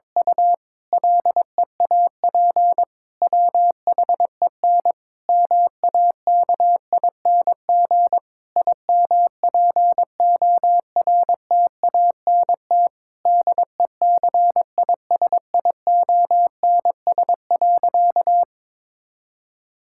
Morse code, heard as a keyed tone, about 22 wpm, 700 Hz. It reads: U LEAP WHEN MAKING IMPORTANT DECISIONS.
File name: Difficult Sentences with Hints - 22wpm